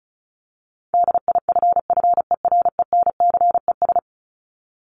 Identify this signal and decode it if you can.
Morse code, heard as a keyed tone, about 34 wpm, 700 Hz. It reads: DIFFERENCES
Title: ES:S Suffix - Top 100 Words with Base Words 35wpm